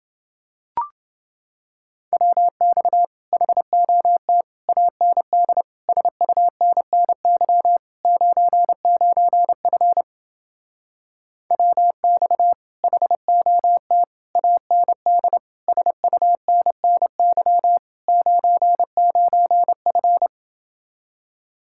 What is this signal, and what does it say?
Morse code, heard as a keyed tone, about 30 wpm, 700 Hz. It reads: WX HOT AND SUNNY 99F WX HOT AND SUNNY 99F